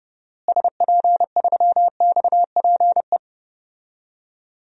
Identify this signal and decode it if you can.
Morse code, heard as a keyed tone, about 31 wpm, 700 Hz. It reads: SP3XPE
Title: DXCC-100 Call Signs - 6 Characters - Repeated 2x - 30wpm